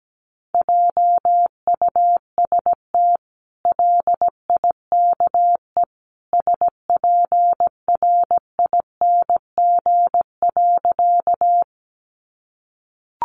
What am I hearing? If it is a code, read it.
Morse code, heard as a keyed tone, about 17 wpm, 700 Hz. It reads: JUST LIKE SPRING.
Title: Sentences from Top 1200 Words 17wpm